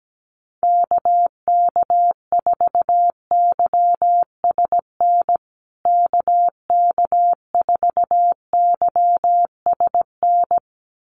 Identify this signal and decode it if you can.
Morse code, heard as a keyed tone, about 17 wpm, 700 Hz. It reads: KK4YSN KK4YSN
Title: USA Call Signs Repeated 2x 17wpm